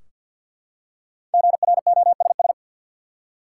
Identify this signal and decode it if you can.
Morse code, heard as a keyed tone, about 51 wpm, 700 Hz. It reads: GROSS
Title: Five Letter Words - 50wpm